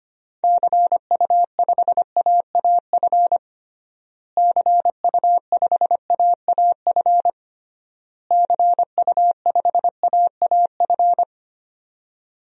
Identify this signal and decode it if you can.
Morse code, heard as a keyed tone, about 25 wpm, 700 Hz. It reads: CU5AAF CU5AAF CU5AAF
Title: DXCC-100 Call Signs - 6 Characters - Repeated 3x - 25wpm